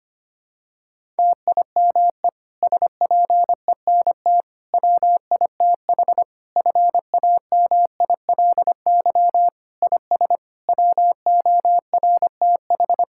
Morse code, heard as a keyed tone, about 25 wpm, 700 Hz. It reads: TIME SPENT WITH FAMILY IS WORTH
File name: practice / 25wpm / Difficult Sentences with Hints - 25wpm